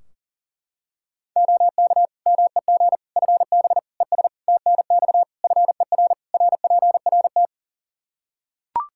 Morse code, heard as a keyed tone, about 40 wpm, 700 Hz. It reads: OK MEG FB ES TNX FER RPRT
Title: Practice QSO - Exchange 2 - Info - 40wpm